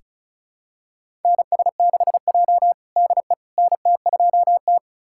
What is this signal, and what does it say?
Morse code, heard as a keyed tone, about 36 wpm, 700 Hz. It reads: NS6J DE NT2T
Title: USA Callsign DE Callsign - 4 Characters - 35wpm